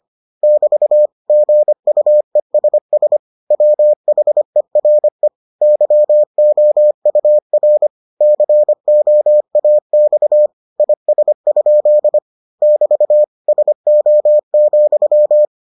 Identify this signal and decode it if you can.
Morse code, heard as a keyed tone, about 25 wpm, 600 Hz. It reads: <BT> GUESS WHERE YOUR COAX IS? <BT> SO,